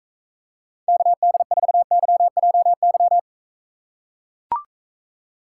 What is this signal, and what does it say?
Morse code, heard as a keyed tone, about 44 wpm, 700 Hz. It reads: KD4YJY